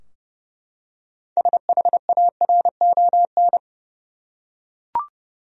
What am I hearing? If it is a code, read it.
Morse code, heard as a keyed tone, about 31 wpm, 700 Hz. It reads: SHARON
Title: USA Common Names - 30wpm